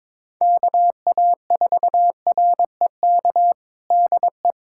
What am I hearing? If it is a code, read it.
Morse code, heard as a keyed tone, about 22 wpm, 700 Hz. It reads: KA4REK DE
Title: USA Callsign DE Callsign 22wpm